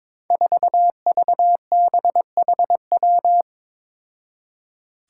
Morse code, heard as a keyed tone, about 22 wpm, 700 Hz. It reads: 4VBHW
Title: Five Challenging Letter-Number Combinations 22wpm